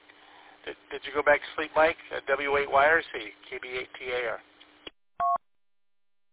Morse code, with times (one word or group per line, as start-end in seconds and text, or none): none